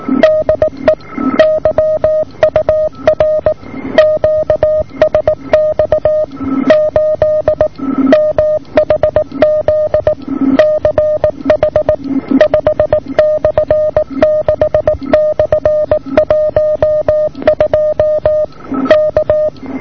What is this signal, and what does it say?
Morse code, heard as a keyed tone, about 18 wpm, 620 Hz. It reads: DE YUR QSX 8 MHZ CH 5/6/12 K
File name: yur3 8700 1105